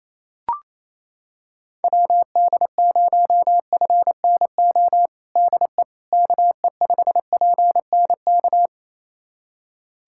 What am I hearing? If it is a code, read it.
Morse code, heard as a keyed tone, about 27 wpm, 700 Hz. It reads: WD0FNO DE KE5PNK
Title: USA Callsign DE Callsign - 6 Characters - 28wpm